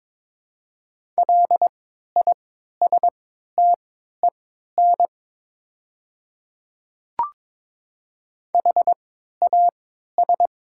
Morse code, heard as a keyed tone, about 22 wpm, spaced as for 12 wpm, 700 Hz. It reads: LISTEN HAS
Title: 3x Character Spacing - Top 500 Words - 22wpm